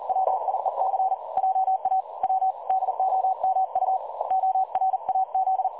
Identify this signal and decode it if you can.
Morse code, heard as a keyed tone, about 41 wpm, 760 Hz. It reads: WIEC JA W DOMU ORAZ
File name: dlazaawansowanych / nagraniazpasma / websdr recording 3565.0kHz